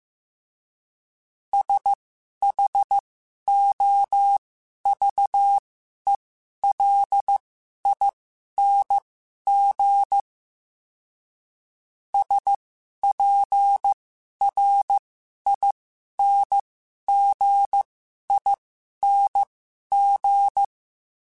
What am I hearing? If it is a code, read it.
Morse code, heard as a keyed tone, about 15 wpm, spaced as for 11 wpm, 800 Hz. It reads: SHOVELING SPRINGING
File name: Special INGs-mp3s / Special INGs-mp3s / Sngl Ing @10